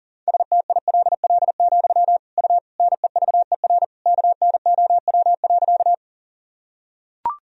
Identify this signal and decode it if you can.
Morse code, heard as a keyed tone, about 40 wpm, 700 Hz. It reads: STILL, U NEVER KNOW.
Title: Sentences from Top 300 Words and Abbreviations 40wpm